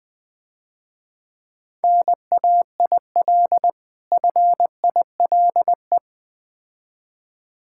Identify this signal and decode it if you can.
Morse code, heard as a keyed tone, about 22 wpm, 700 Hz. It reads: NAIL FILE
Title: Difficult Words - Speed-Racing with Hints - 20wpm